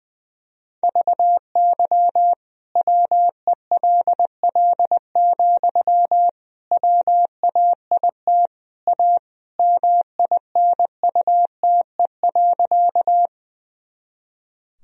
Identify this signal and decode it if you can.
Morse code, heard as a keyed tone, about 20 wpm, 700 Hz. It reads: VY WELL, WAIT A MINUTE.
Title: Sentences from Top 500 Words and Abbreviations 20wpm